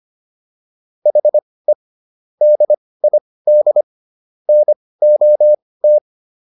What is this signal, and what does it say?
Morse code, heard as a keyed tone, about 25 wpm, spaced as for 18 wpm, 600 Hz. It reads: HE DID NOT